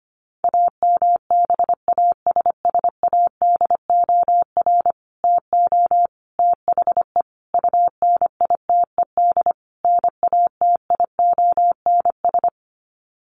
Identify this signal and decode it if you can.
Morse code, heard as a keyed tone, about 25 wpm, 700 Hz. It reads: AMBASSADOR TO THE UNITED NATIONS
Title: Sets of 5 Words Encore2 25wpm